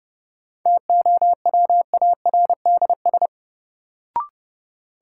Morse code, heard as a keyed tone, about 30 wpm, 700 Hz. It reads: TOWARDS
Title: Sentences from Top 1000 Words - Review Part 2 - New Word Drill - 30wpm